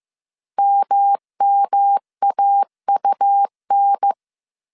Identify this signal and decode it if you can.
Morse code, heard as a keyed tone, about 15 wpm, 800 Hz. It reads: MMAUN
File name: CW / DET / 1 3 1500C S A 14wpm 2447